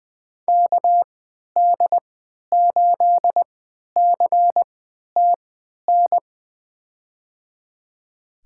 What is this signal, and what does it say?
Morse code, heard as a keyed tone, about 20 wpm, spaced as for 11 wpm, 700 Hz. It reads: KD8CTN
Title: USA Call Signs - 3x Character Spacing - 20wpm